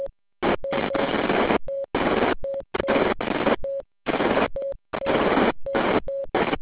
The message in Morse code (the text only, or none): none